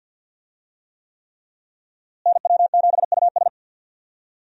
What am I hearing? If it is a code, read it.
Morse code, heard as a keyed tone, about 46 wpm, 700 Hz. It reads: NW7US
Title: USA Call Signs - 5 Characters - Repeated 3x - 50wpm